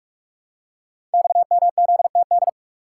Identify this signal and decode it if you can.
Morse code, heard as a keyed tone, about 44 wpm, 700 Hz. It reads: XMZTD
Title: Code Groups - 5 Letter:Numbers - 45wpm